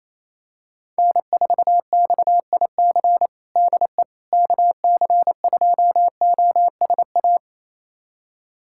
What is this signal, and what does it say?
Morse code, heard as a keyed tone, about 28 wpm, 700 Hz. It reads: N4XIC DE KC2OSA
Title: USA Callsign DE Callsign 28wpm